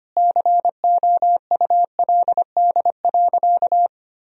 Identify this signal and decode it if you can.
Morse code, heard as a keyed tone, about 25 wpm, 700 Hz. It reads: COULD.
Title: Sentences from Top 1000 Words - Review Part 3 - 25wpm